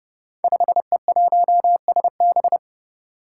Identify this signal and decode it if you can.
Morse code, heard as a keyed tone, about 30 wpm, 700 Hz. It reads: 5E1SB
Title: Five Challenging Letter-Number Combinations 30wpm